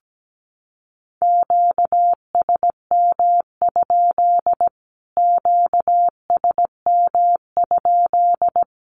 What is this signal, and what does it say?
Morse code, heard as a keyed tone, about 17 wpm, 700 Hz. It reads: QSM? QSM?
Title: Q-Codes Repeated 3x 17wpm